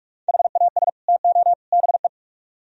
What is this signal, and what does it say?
Morse code, heard as a keyed tone, about 44 wpm, 700 Hz. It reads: HAS TO BE